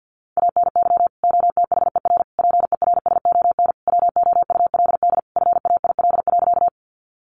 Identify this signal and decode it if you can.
Morse code, heard as a keyed tone, about 50 wpm, 700 Hz. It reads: ANY OTHER PERSON WOULD FAIL.
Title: Sentences from Top 1100 Words 50wpm